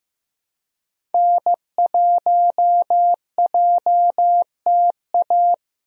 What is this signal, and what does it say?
Morse code, heard as a keyed tone, about 15 wpm, 700 Hz. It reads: N1JTA
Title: USA Call Signs - 5 Characters - 15wpm